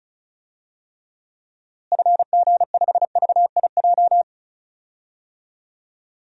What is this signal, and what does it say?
Morse code, heard as a keyed tone, about 36 wpm, 700 Hz. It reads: FG5VIJ
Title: DXCC-100 Call Signs - 6 Characters - Repeated 2x - 35wpm